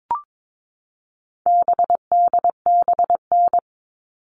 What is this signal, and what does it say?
Morse code, heard as a keyed tone, about 22 wpm, 700 Hz. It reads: BDBN
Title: All Cut Numbers - 4-Digit 22wpm